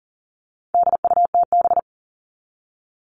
Morse code, heard as a keyed tone, about 40 wpm, 700 Hz. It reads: DUTB
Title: All Cut Numbers - 4-Digit 40wpm